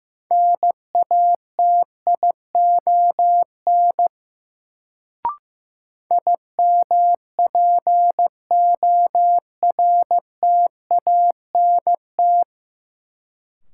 Morse code, heard as a keyed tone, about 15 wpm, 700 Hz. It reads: NATION IMPORTANT
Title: Sets of 3 Words - New Word Drill - 15wpm